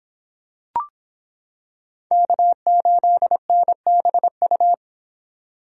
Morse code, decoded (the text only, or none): K8NBU